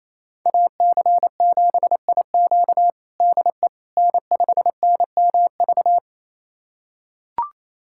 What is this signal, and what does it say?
Morse code, heard as a keyed tone, about 28 wpm, 700 Hz. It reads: AC7IQ DE N5NMV
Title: USA Callsign DE Callsign - 5 Characters - 28wpm